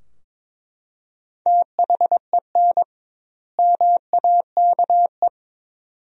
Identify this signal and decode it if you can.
Morse code, heard as a keyed tone, about 22 wpm, 700 Hz. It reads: THEN MAKE
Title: 2x Word-Spacing - Sentences from Top 100 Words - 22wpm